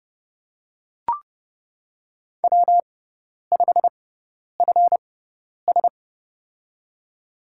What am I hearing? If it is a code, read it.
Morse code, heard as a keyed tone, about 32 wpm, spaced as for 10 wpm, 700 Hz. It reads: W5FS
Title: USA Call Signs - 4 Characters - 6x Character Spacing - 30wpm